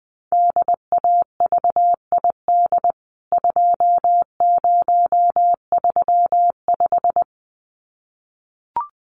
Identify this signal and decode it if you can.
Morse code, heard as a keyed tone, about 20 wpm, 700 Hz. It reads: DAVID 2035
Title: CWT Exchange 20wpm